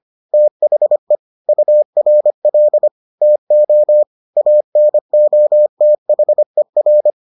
Morse code, heard as a keyed tone, about 25 wpm, 600 Hz. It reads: THE URL TO ANOTHER